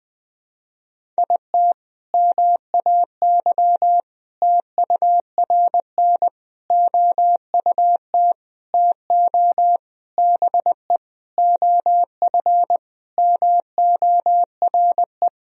Morse code, heard as a keyed tone, about 20 wpm, 700 Hz. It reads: IT MAY TURN OUT TO BE OF MORE
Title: Sentences from Top 400 Words and Abbreviations 20wpm